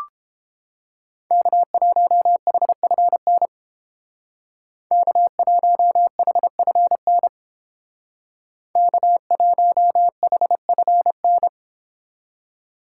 Morse code, heard as a keyed tone, about 29 wpm, 700 Hz. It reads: K1HFN K1HFN K1HFN